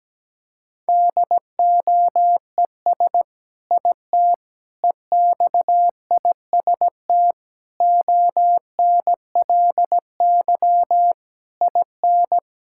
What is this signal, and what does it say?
Morse code, heard as a keyed tone, about 17 wpm, 700 Hz. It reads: DOES IT EXIST ONLY IN
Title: Sentences from Top 1000 Words - Review Part 5 - 17wpm